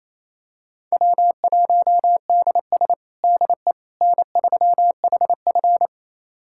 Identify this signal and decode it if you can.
Morse code, heard as a keyed tone, about 28 wpm, 700 Hz. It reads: W1DS DE N3HF
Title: USA Callsign DE Callsign - 4 Characters - 28wpm